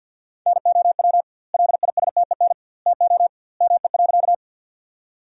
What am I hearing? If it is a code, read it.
Morse code, heard as a keyed tone, about 52 wpm, 700 Hz. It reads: NOW LISTEN TO ME.
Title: 2x Word-Spacing - Sentences from Top 400 Words - 50wpm